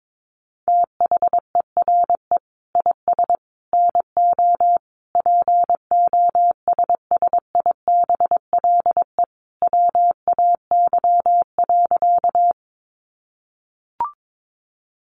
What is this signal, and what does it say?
Morse code, heard as a keyed tone, about 22 wpm, 700 Hz. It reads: THERE IS NO POSSIBLE WAY.
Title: Sentences from Top 500 Words 22wpm